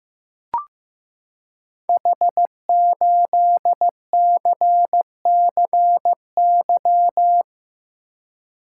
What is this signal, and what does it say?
Morse code, heard as a keyed tone, about 15 wpm, 700 Hz. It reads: H8CCY
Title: Code Groups - 5 Letter:Numbers - 15wpm